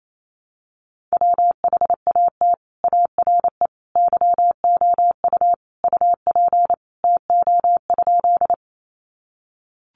Morse code, heard as a keyed tone, about 28 wpm, 700 Hz. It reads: WHAT ARE YOU UP TO?
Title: Sentences from Top 100 Words 28wpm